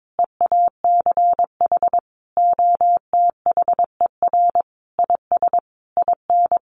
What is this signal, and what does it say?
Morse code, heard as a keyed tone, about 22 wpm, 700 Hz. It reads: EACH OTHER IS IN
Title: Sentences from Top 1500 Words 22wpm